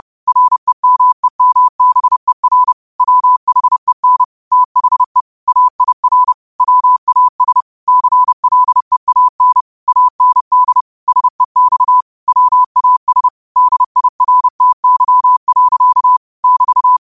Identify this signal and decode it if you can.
Morse code, heard as a keyed tone, about 30 wpm, 1000 Hz. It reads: REMEMBER WHEN THE AIR WAS CLEAN AND SEX WAS DIRTY. <BT>